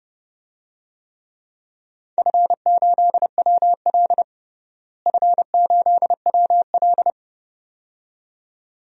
Morse code, heard as a keyed tone, about 30 wpm, 700 Hz. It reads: F8WL F8WL
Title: DXCC-100 Call Signs - 4 Characters - Repeated 2x - 30wpm